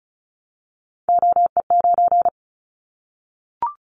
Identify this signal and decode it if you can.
Morse code, heard as a keyed tone, about 36 wpm, 700 Hz. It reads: OE9
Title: Callsign Prefixes for Top 25 DXCC - 35wpm